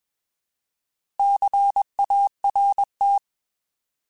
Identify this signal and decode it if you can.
Morse code, heard as a keyed tone, about 21 wpm, 800 Hz. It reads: CART
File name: Alphabetical / Alphabetical mp3s / Cs @20